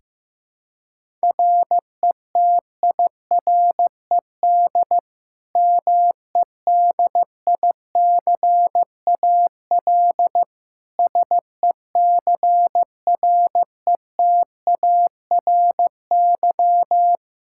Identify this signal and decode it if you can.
Morse code, heard as a keyed tone, about 15 wpm, 700 Hz. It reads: RETIRED MEDICAL SECRETARY
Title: Practice QSO - Exchange 3 - More Info - 15wpm